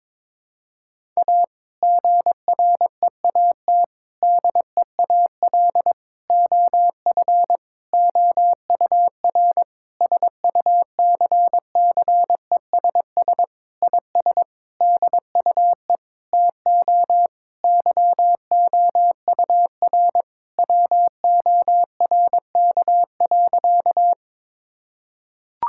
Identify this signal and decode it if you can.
Morse code, heard as a keyed tone, about 22 wpm, 700 Hz. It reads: A GREAT DEAL OF OUR SUCCESS IS DUE TO YOUR WORK.
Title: Sentences from Top 1000 Words - Review Part 6 - 22wpm